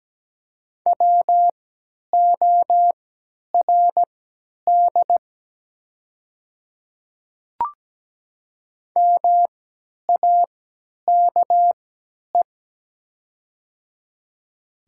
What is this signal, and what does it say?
Morse code, heard as a keyed tone, about 17 wpm, spaced as for 10 wpm, 700 Hz. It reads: WORD MAKE
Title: 3x Character Spacing - Top 200 Words - 17wpm